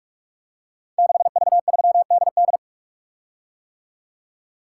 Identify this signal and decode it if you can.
Morse code, heard as a keyed tone, about 44 wpm, 690 Hz. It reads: 6V3DD
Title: Code Groups - 5 Letter:Numbers - 45wpm